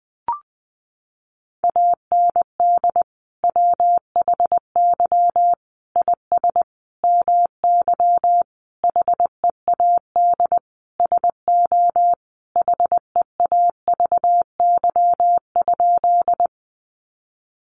Morse code, heard as a keyed tone, about 19 wpm, 700 Hz. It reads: AND WHY IS MY HEAD SO HEAVY?
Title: Sentences from Top 1100 Words 20wpm